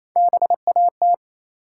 Morse code, heard as a keyed tone, about 27 wpm, 700 Hz. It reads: BAT
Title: Three Letter Words - 28wpm